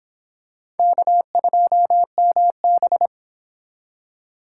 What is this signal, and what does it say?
Morse code, heard as a keyed tone, about 26 wpm, 700 Hz. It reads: K2MB